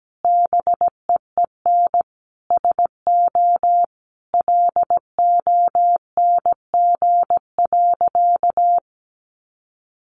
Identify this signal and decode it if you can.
Morse code, heard as a keyed tone, about 17 wpm, 700 Hz. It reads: BEEN SO LONG.
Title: Sentences from Top 100 Words 17wpm